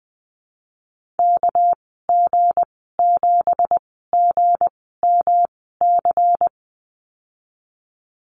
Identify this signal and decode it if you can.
Morse code, heard as a keyed tone, about 20 wpm, spaced as for 15 wpm, 700 Hz. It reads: KG7GMC
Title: 238 Lesson 38g - Letter q - Call Signs 20wpm